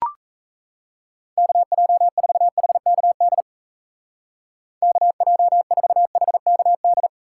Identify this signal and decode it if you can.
Morse code, heard as a keyed tone, about 36 wpm, 700 Hz. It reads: KJ4HKD KJ4HKD